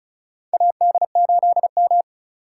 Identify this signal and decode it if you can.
Morse code, heard as a keyed tone, about 34 wpm, 700 Hz. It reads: AD8M
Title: USA Call Signs - 4 Characters - 2x Character Spacing - 35wpm